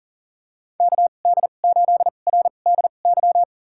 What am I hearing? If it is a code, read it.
Morse code, heard as a keyed tone, about 39 wpm, spaced as for 29 wpm, 700 Hz. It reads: KD8RDY
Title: USA Call Signs - 6 Characters - 2x Character Spacing - 40wpm